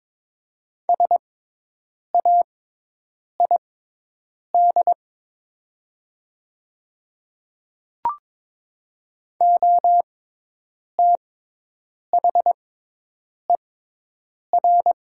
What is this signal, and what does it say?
Morse code, heard as a keyed tone, about 22 wpm, spaced as for 8 wpm, 700 Hz. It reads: SAID OTHER
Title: 6x Character Spacing - Top 100 Words - 22wpm